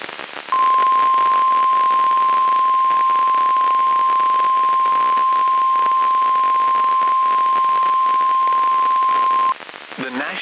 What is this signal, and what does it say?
Emergency Alert System (SAME) tones (the 1050 Hz attention tone), with noise. It is heard from 0.5 to 9.6 s.
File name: eas messages / EAS WXM95 Towanda